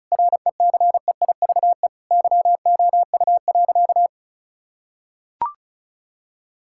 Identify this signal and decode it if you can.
Morse code, heard as a keyed tone, about 34 wpm, 700 Hz. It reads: RECEIVE YOU.
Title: Sentences from Top 1600 Words 35wpm